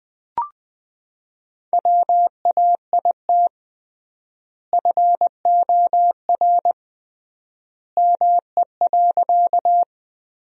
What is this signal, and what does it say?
Morse code, heard as a keyed tone, about 20 wpm, 700 Hz. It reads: WAIT FOR ME.
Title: 3x Word-Spacing - Sentences from Top 500 Words - 20wpm